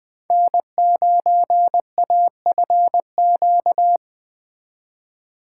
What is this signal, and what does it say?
Morse code, heard as a keyed tone, about 20 wpm, 700 Hz. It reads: N9AFQ